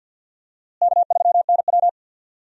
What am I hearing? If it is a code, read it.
Morse code, heard as a keyed tone, about 50 wpm, 700 Hz. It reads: K3NW
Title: USA Call Signs - 4 Characters - Repeated 2x - 50wpm